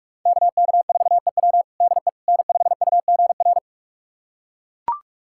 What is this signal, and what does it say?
Morse code, heard as a keyed tone, about 41 wpm, 700 Hz. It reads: KK4EW DE N5UGR
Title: USA Callsign DE Callsign - 5 Characters - 45wpm